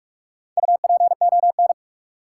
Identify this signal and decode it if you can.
Morse code, heard as a keyed tone, about 44 wpm, 700 Hz. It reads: UPON
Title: Sentences from Top 1000 Words - Review Part 6 - New Word Drill - 45wpm